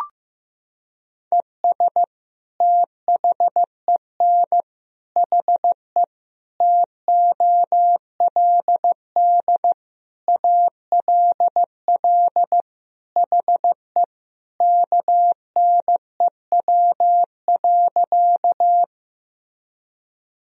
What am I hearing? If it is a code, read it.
Morse code, heard as a keyed tone, about 15 wpm, 700 Hz. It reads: ES THEN HE TOLD ALL HE KNEW.
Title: Sentences from Top 400 Words and Abbreviations 15wpm